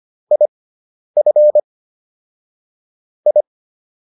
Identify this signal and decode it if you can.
Morse code, heard as a keyed tone, about 25 wpm, spaced as for 10 wpm, 600 Hz. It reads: IF I